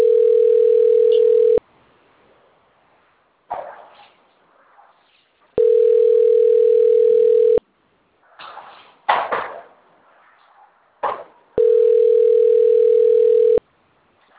Ringback tone (North America), 440 + 480 Hz.